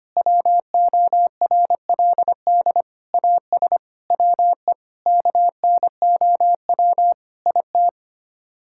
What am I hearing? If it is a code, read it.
Morse code, heard as a keyed tone, about 25 wpm, 700 Hz. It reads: WORLD AS WE KNOW IT
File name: Sets of 9 Words - 25wpm